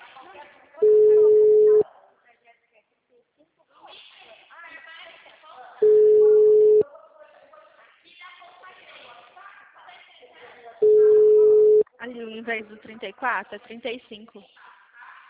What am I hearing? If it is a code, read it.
Ringback tone (Europe, 425 Hz), with speech.
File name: out- -2017- - - .37791